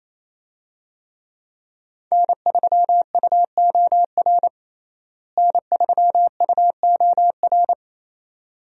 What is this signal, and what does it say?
Morse code, heard as a keyed tone, about 28 wpm, 700 Hz. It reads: N3UOR N3UOR
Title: USA Call Signs - 5 Characters - Repeated 2x - 28wpm